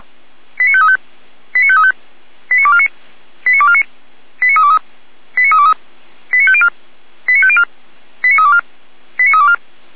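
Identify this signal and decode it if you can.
Five-tone selective call (ZVEI), heard as quick runs of five tones, tones of about 70 ms. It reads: 89535 89535 89249 89249 89323 89323 89504 89504 89324 89324
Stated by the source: ZVEI five-tone selective call received by SDR.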